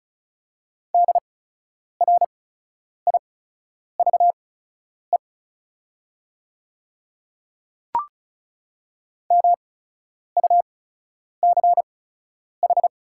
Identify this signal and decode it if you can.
Morse code, heard as a keyed tone, about 36 wpm, spaced as for 10 wpm, 690 Hz. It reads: DRIVE MUCH
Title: 8x Character Spacing - Top 500 Words - 35wpm